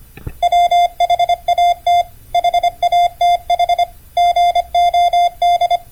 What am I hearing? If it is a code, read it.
Morse code, heard as a keyed tone, about 25 wpm, 690 Hz. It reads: WHAT HATH GOD